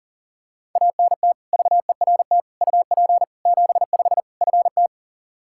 Morse code, heard as a keyed tone, about 40 wpm, 700 Hz. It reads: ANT VERT UP 75 FT
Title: K1USN SST QSO Practice 40wpm